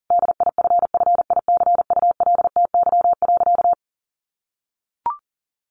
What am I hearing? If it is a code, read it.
Morse code, heard as a keyed tone, about 37 wpm, 700 Hz. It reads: DIFFICULTY.
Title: Sentences from Top 1500 Words 40wpm